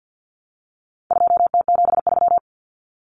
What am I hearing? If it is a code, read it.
Morse code, heard as a keyed tone, about 50 wpm, 700 Hz. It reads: 2T73
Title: Common Cut Numbers - 4-Digit 50wpm